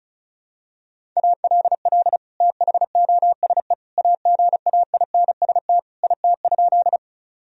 Morse code, heard as a keyed tone, about 36 wpm, 700 Hz. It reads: ALL THOSE AGAINST IT?
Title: Sentences from Top 1000 Words - Review Part 1 - 35wpm